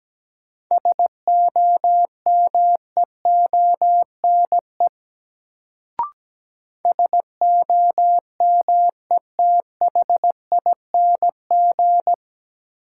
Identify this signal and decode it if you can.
Morse code, heard as a keyed tone, about 17 wpm, 700 Hz. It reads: SOMEONE SOMETHING
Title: Sentences from Top 600 Words - New Word Drill - 17wpm